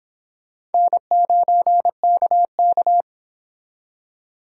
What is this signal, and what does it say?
Morse code, heard as a keyed tone, about 26 wpm, 700 Hz. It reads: N9KK